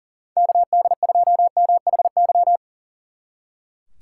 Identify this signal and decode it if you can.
Morse code, heard as a keyed tone, about 49 wpm, 700 Hz. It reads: KD2MHY
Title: Speed Racing - USA 2x3 Call Signs 40wpm